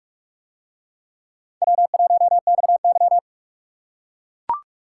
Morse code, heard as a keyed tone, about 44 wpm, 700 Hz. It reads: W1XY